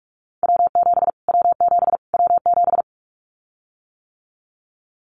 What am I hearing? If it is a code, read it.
Morse code, heard as a keyed tone, about 44 wpm, 700 Hz. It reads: W7 W7 W7
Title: Callsign Prefixes for Top 25 DXCC - Repeated 3x - 45wpm